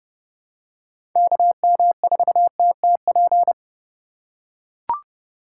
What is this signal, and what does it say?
Morse code, heard as a keyed tone, about 28 wpm, 700 Hz. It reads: KM4TTP